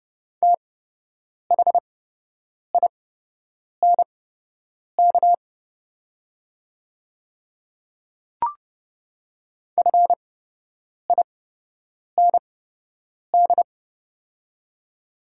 Morse code, heard as a keyed tone, about 30 wpm, spaced as for 8 wpm, 700 Hz. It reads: THINK FIND
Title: 8x Character Spacing - Top 300 Words - 30wpm